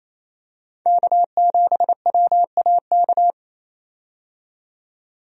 Morse code, heard as a keyed tone, about 28 wpm, 700 Hz. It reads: K7WAK
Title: USA Call Signs - 5 Characters - 28wpm